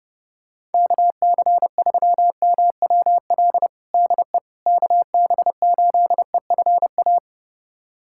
Morse code, heard as a keyed tone, about 29 wpm, 700 Hz. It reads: KC3MWL DE KB8EFA
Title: USA Callsign DE Callsign - 6 Characters - 30wpm